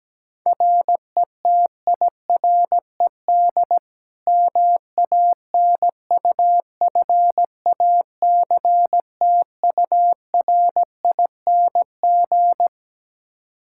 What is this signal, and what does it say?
Morse code, heard as a keyed tone, about 17 wpm, 700 Hz. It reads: RETIRED MANUFACTURING
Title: Practice QSO - Exchange 3 - More Info - 17wpm